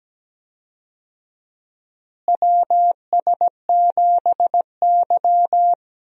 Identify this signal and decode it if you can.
Morse code, heard as a keyed tone, about 17 wpm, 700 Hz. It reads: WS7Y
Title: USA Call Signs - 4 Characters - Repeated 2x - 17wpm